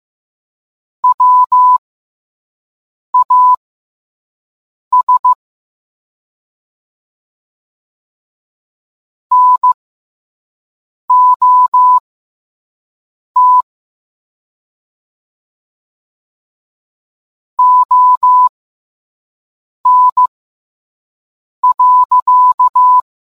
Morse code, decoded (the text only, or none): WAS NOT ON.